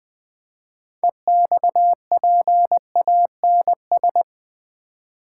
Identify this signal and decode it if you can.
Morse code, heard as a keyed tone, about 20 wpm, 700 Hz. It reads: EXPANS
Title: Build A Word - 3 Letters At A Time - 9-Letter Words - 20wpm